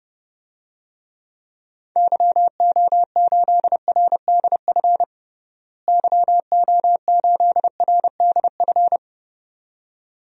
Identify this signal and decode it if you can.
Morse code, heard as a keyed tone, about 30 wpm, 700 Hz. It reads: YO8RDF YO8RDF
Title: DXCC-100 Call Signs - 6 Characters - Repeated 2x - 30wpm